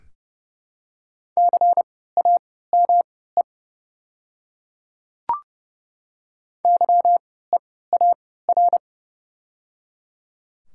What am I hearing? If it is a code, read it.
Morse code, heard as a keyed tone, about 30 wpm, spaced as for 17 wpm, 690 Hz. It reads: CAME YEAR
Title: 3x Character Spacing - Top 500 Words - 30wpm